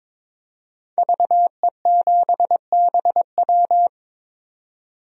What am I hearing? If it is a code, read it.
Morse code, heard as a keyed tone, about 22 wpm, 700 Hz. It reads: VE7BW
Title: DXCC-100 Call Signs - 5 Characters - 22wpm